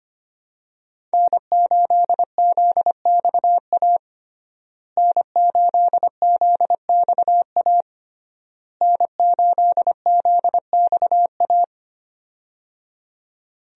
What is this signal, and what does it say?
Morse code, heard as a keyed tone, about 25 wpm, 700 Hz. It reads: N8ZXA N8ZXA N8ZXA
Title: USA Call Signs - 5 Characters - Repeated 3x - 25wpm